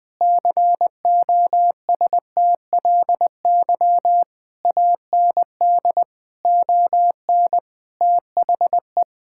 Morse code, heard as a keyed tone, about 20 wpm, 700 Hz. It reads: COSTLY AND ON THE
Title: Sentences with 8-Word Phrases - 20wpm